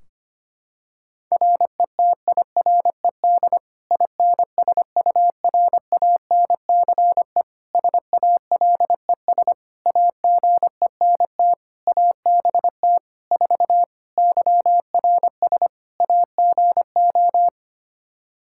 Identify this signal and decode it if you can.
Morse code, heard as a keyed tone, about 25 wpm, 700 Hz. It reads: RETIRED INSURANCE SALES AGENT ABT 4 YRS AGO
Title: K1USN SST QSO Practice 25wpm